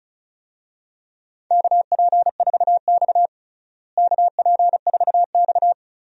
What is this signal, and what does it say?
Morse code, heard as a keyed tone, about 36 wpm, 700 Hz. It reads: KP4X KP4X
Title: DXCC-100 Call Signs - 4 Characters - Repeated 3x - 35wpm